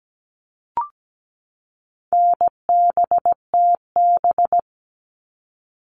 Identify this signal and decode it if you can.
Morse code, heard as a keyed tone, about 17 wpm, 700 Hz. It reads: NBTB
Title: All Cut Numbers - 4-Digit 17wpm